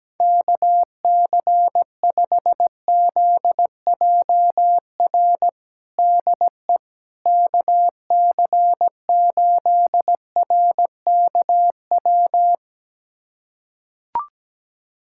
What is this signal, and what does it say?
Morse code, heard as a keyed tone, about 17 wpm, 700 Hz. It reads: KC5ZJR DE KC8RKW
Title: USA Callsign DE Callsign - 6 Characters - 17wpm